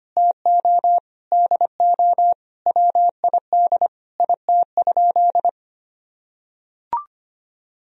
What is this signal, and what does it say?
Morse code, heard as a keyed tone, about 25 wpm, 700 Hz. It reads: TO DO WID IT?
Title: Sentences from Top 100 Words and Abbreviations 25wpm